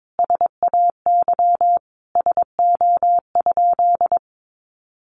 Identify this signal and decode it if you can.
Morse code, heard as a keyed tone, about 22 wpm, 700 Hz. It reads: SAY SO?
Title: Sentences from Top 1200 Words 22wpm